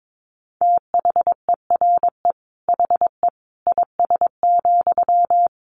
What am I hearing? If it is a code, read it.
Morse code, heard as a keyed tone, about 22 wpm, 700 Hz. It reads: THERE HE IS,
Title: Sentences from Top 100 Words 22wpm